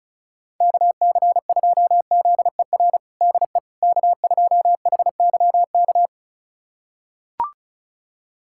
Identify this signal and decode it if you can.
Morse code, heard as a keyed tone, about 36 wpm, 700 Hz. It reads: KC2ZER DE K2HYK
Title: USA Callsign DE Callsign 35wpm